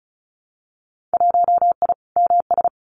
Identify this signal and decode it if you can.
Morse code, heard as a keyed tone, about 34 wpm, 700 Hz. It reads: 1I MS
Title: Winter Field Day Exchanges Encore 35wpm